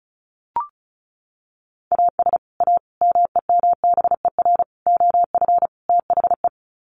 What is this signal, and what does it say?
Morse code, heard as a keyed tone, about 34 wpm, 700 Hz. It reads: AS A MEMBER OF THE
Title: Sets of 5 Words 35wpm